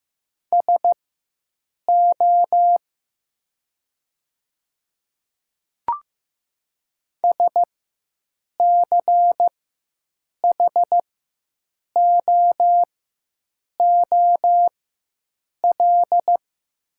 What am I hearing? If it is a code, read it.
Morse code, heard as a keyed tone, about 15 wpm, spaced as for 7 wpm, 700 Hz. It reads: SO SCHOOL
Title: 4x Character Spacing - Top 500 Words - 15wpm